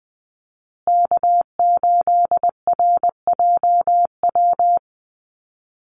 Morse code, heard as a keyed tone, about 20 wpm, 700 Hz. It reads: K8RJW
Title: Call Signs Repeated 3x - 20wpm